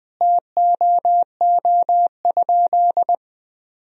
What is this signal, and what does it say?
Morse code, heard as a keyed tone, about 20 wpm, 700 Hz. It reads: TOO?
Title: Sentences from Top 1000 Words - Review Part 2 - 20wpm